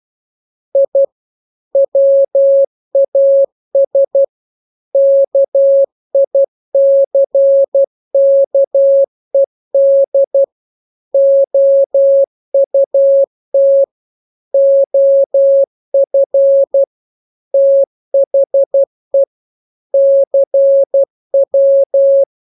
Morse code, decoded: I WAS KICKED OUT OF THE CW